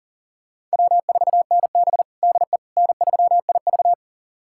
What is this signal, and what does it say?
Morse code, heard as a keyed tone, about 40 wpm, 700 Hz. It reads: W4NB DE N3IV
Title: USA Callsign DE Callsign - 4 Characters - 40wpm